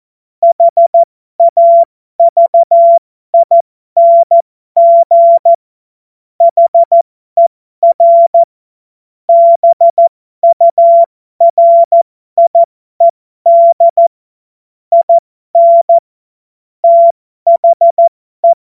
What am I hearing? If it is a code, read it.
Morse code, heard as a keyed tone, about 14 wpm, spaced as for 12 wpm, 680 Hz. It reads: HAVING HER BURIED IN THE
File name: Morsekurs / Witze